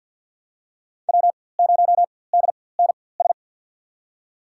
Morse code, heard as a keyed tone, about 49 wpm, spaced as for 23 wpm, 700 Hz. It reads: W0DNS
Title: USA Call Signs - 5 Characters - 4x Character Spacing - 50wpm